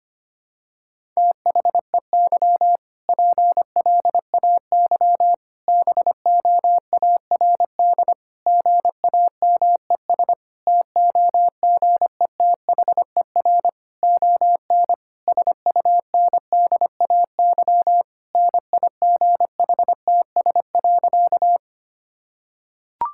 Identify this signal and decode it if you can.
Morse code, heard as a keyed tone, about 25 wpm, 700 Hz. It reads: THEY PLAY BOARD GAMES TOGETHER ON SUNDAY NIGHTS.
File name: practice / 25wpm / Difficult Sentences with Hints - 25wpm